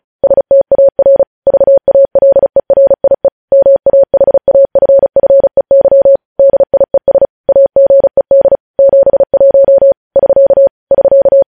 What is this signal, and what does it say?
Morse code, heard as a keyed tone, about 36 wpm, 550 Hz. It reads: STAR VALERIE MAHAFFEY DIES AGED 71 <SK> <SK>
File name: podcast / News headlines-1x-35x35